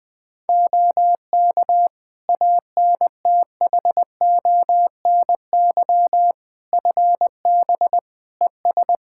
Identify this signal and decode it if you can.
Morse code, heard as a keyed tone, about 20 wpm, 700 Hz. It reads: OK ANTHONY FB ES
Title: Practice QSO - Exchange 2 - Info - 20wpm